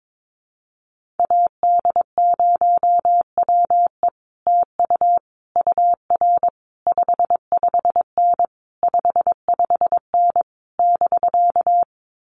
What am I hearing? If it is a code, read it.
Morse code, heard as a keyed tone, about 22 wpm, 700 Hz. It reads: AD0WE TU UR 55N 55N <BK>
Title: Common QSO Phrases Encore 22wpm